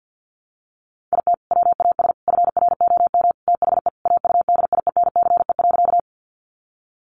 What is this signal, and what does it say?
Morse code, heard as a keyed tone, about 50 wpm, 700 Hz. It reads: IT WAS FROM THE AUDIENCE.
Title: Sentences from Top 800 Words 50wpm